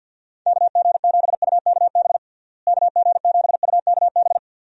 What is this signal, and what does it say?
Morse code, heard as a keyed tone, about 51 wpm, 700 Hz. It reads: KG7UKB KG7UKB
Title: USA Call Signs - 6 Characters - Repeated 3x - 50wpm